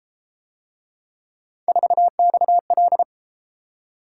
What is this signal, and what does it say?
Morse code, heard as a keyed tone, about 33 wpm, 700 Hz. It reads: 4XL